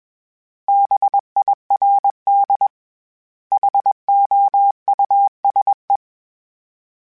Morse code, heard as a keyed tone, about 21 wpm, 800 Hz. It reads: BIRD HOUSE
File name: General / General-mp3s / Houses @20